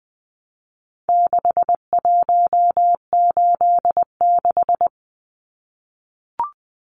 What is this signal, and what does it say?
Morse code, heard as a keyed tone, about 20 wpm, 700 Hz. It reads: 6186
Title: Numbers - 4-Digit 20wpm